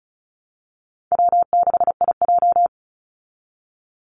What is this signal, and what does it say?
Morse code, heard as a keyed tone, about 34 wpm, 700 Hz. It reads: W6IJ
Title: Easy Call Signs Repeated 2x - 35wpm